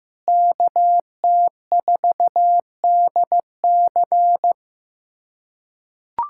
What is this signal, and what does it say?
Morse code, heard as a keyed tone, about 16 wpm, 700 Hz. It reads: KT4DC
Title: USA Callsign DE Callsign - 5 Characters - 15wpm